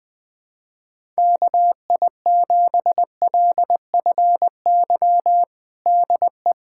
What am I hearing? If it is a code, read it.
Morse code, heard as a keyed tone, about 20 wpm, 700 Hz. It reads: KI7LFY DE
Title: USA Callsign DE Callsign 20wpm